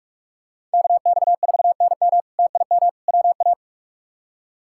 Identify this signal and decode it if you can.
Morse code, heard as a keyed tone, about 43 wpm, 700 Hz. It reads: KX4NM TIM WA
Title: K1USN SST QSO Practice - 45wpm